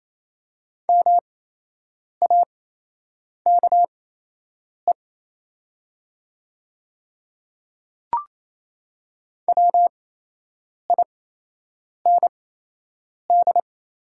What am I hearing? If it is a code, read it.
Morse code, heard as a keyed tone, about 28 wpm, spaced as for 8 wpm, 690 Hz. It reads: MAKE WIND
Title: 8x Character Spacing - Top 500 Words - 28wpm